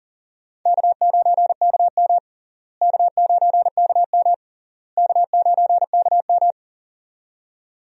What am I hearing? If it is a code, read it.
Morse code, heard as a keyed tone, about 40 wpm, 700 Hz. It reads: K9KM K9KM K9KM
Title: USA Call Signs - 4 Characters - Repeated 3x - 40wpm